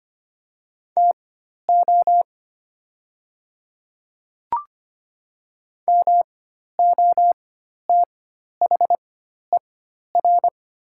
Morse code, heard as a keyed tone, about 25 wpm, spaced as for 12 wpm, 700 Hz. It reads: TO MOTHER
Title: 4x Character Spacing - Top 300 Words - 25wpm